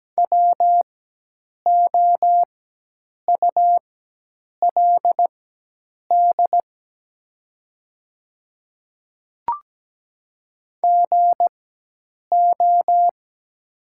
Morse code, heard as a keyed tone, about 17 wpm, spaced as for 8 wpm, 700 Hz. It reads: WOULD GO
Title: 4x Character Spacing - Top 100 Words - 17wpm